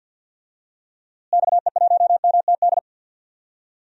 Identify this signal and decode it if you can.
Morse code, heard as a keyed tone, about 51 wpm, 700 Hz. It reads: XE1MTD
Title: DXCC-100 Call Signs - 6 Characters - Repeated 2x - 50wpm